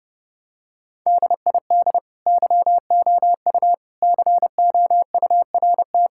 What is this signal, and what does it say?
Morse code, heard as a keyed tone, about 32 wpm, 700 Hz. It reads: DID YOU COURT
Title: Sentences from Top 1600 Words 30wpm